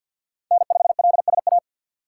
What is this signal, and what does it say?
Morse code, heard as a keyed tone, about 52 wpm, 700 Hz. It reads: N5LSA
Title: USA Call Signs - 5 Characters - 3x Character Spacing - 50wpm